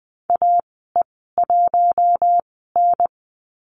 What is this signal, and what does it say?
Morse code, heard as a keyed tone, about 20 wpm, spaced as for 14 wpm, 700 Hz. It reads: AE1N
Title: 042 Lesson 8g - Number 1 - Call Signs 20wpm